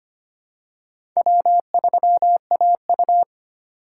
Morse code, heard as a keyed tone, about 25 wpm, 700 Hz. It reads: W3AU
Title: USA Call Signs - 4 Characters - 2x Character Spacing - 25wpm